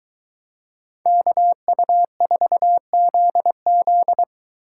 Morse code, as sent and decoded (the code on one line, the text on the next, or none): -.- ..- ....- --.. --..
KU4ZZ